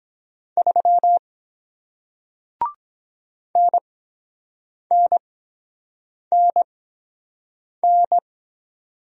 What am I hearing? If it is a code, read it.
Morse code, heard as a keyed tone, about 24 wpm, spaced as for 7 wpm, 700 Hz. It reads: 3 NNNN